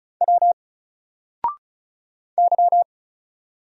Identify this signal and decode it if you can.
Morse code, heard as a keyed tone, about 36 wpm, 700 Hz. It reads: W Y
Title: Single Letter-Number Alt 35wpm